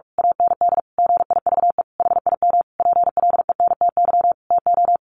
Morse code, heard as a keyed tone, about 44 wpm, 700 Hz. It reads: AND GIVE HIM PLENTY TO
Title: Sentences from Top 1300 Words 45wpm